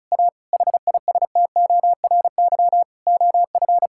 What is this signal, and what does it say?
Morse code, heard as a keyed tone, about 36 wpm, 700 Hz. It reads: A HISTORY OF
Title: Sentences from Top 1000 Words - Review Part 5 - 35wpm